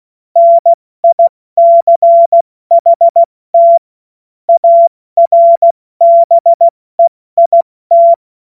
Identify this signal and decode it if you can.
Morse code, heard as a keyed tone, about 16 wpm, 680 Hz. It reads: NICHT ARBEIT
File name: Witz-16-September-2025-70-F80